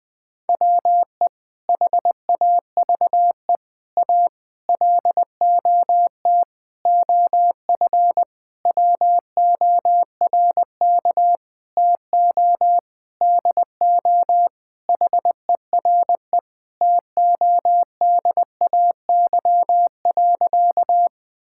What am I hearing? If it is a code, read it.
Morse code, heard as a keyed tone, about 20 wpm, 700 Hz. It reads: WE HAVE A LOT OF WORK TO DO HERE TODAY.
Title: Sentences with 7-Word Phrases - 20wpm